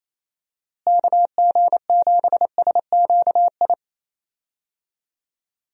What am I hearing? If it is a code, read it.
Morse code, heard as a keyed tone, about 27 wpm, 700 Hz. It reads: KG7SQI
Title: USA Call Signs - 6 Characters - Repeated 3x - 28wpm